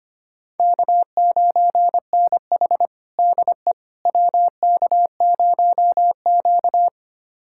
Morse code, heard as a keyed tone, about 25 wpm, 700 Hz. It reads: K9NH DE WK0Q
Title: USA Callsign DE Callsign - 4 Characters - 25wpm